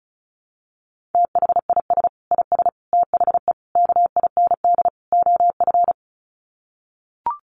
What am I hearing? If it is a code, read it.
Morse code, heard as a keyed tone, about 34 wpm, 700 Hz. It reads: THIS IS THE KIND OF
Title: Sets of 5 Words Encore 35wpm